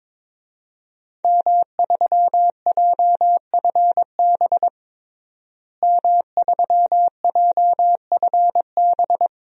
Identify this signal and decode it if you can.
Morse code, heard as a keyed tone, about 22 wpm, 700 Hz. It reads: M3JFB M3JFB
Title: DXCC-100 Call Signs - 5 Characters - Repeated 2x - 22wpm